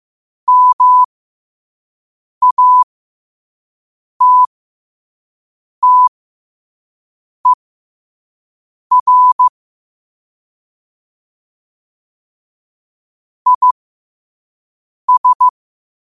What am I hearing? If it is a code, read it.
Morse code, heard as a keyed tone, about 15 wpm, spaced as for 5 wpm, 1000 Hz. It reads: MATTER IS
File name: qotdfiles / 5wpm